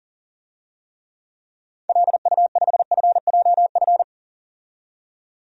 Morse code, heard as a keyed tone, about 40 wpm, 700 Hz. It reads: LU5FJF
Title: DXCC-100 Call Signs - 6 Characters - 40wpm